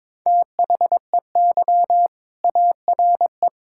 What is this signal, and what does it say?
Morse code, heard as a keyed tone, about 22 wpm, 700 Hz. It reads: THEY ARE
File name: practice / 22wpm / Sentences with 2-Word Phrases - 22wpm